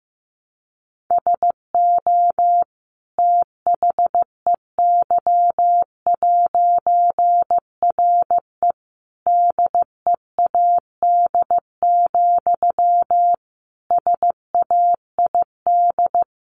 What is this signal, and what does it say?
Morse code, heard as a keyed tone, about 15 wpm, 700 Hz. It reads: SO THEY'RE DEAD, SAID
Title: A Sound Of Thunder 15wpm